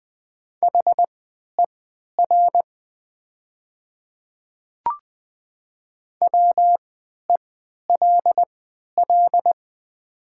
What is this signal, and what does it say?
Morse code, heard as a keyed tone, about 20 wpm, spaced as for 11 wpm, 700 Hz. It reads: HER WELL
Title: 3x Character Spacing - Top 200 Words - 20wpm